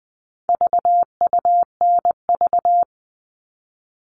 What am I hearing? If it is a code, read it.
Morse code, heard as a keyed tone, about 20 wpm, 700 Hz. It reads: VUNV
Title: All Cut Numbers - 4-Digit 20wpm